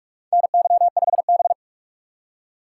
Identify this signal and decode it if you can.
Morse code, heard as a keyed tone, about 44 wpm, 700 Hz. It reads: NY5B
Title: USA Call Signs - 4 Characters - 6x Character Spacing - 45wpm